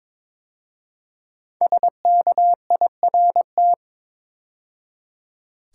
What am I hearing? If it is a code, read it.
Morse code, heard as a keyed tone, about 23 wpm, 700 Hz. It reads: SKIRT
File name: 5 Letter Words - First 3 Letters Given - 22wpm